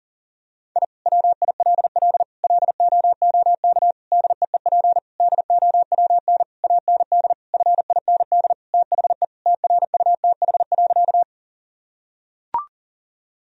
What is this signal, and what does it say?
Morse code, heard as a keyed tone, about 40 wpm, 700 Hz. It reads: I WILL LOOK DEEP DOWN AND FIND THE TRUTH.
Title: Sentences from Top 1000 Words - Review Part 3 - 40wpm